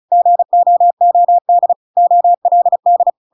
Morse code, heard as a keyed tone, about 34 wpm, 700 Hz. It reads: GOOD OLD
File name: Pinocchio / Pinocchio ch7 35 35 ik1hge